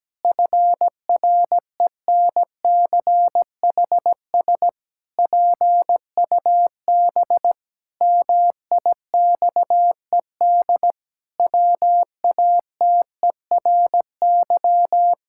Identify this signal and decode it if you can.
Morse code, heard as a keyed tone, about 17 wpm, 700 Hz. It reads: FRENCHS PUB MIXED WATERY
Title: Pangram Challenge v2 - 17wpm